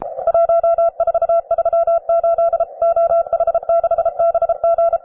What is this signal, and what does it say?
Morse code, heard as a keyed tone, about 33 wpm, 670 Hz. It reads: 1438 OH6BG
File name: wae / cw advanced / rw9sw cw